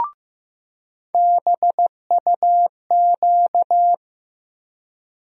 Morse code, heard as a keyed tone, about 15 wpm, 700 Hz. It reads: BUQ
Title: USA Call Sign Postfixes - 3 Letters 15wpm